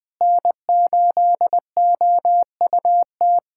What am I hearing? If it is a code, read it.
Morse code, heard as a keyed tone, about 20 wpm, 700 Hz. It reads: N8OUT
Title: USA Callsign DE Callsign - 5 Characters - 20wpm